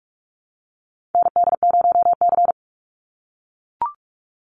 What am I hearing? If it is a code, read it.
Morse code, heard as a keyed tone, about 44 wpm, 700 Hz. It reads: ND0C
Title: NA SOTA Chaser Callsigns 45wpm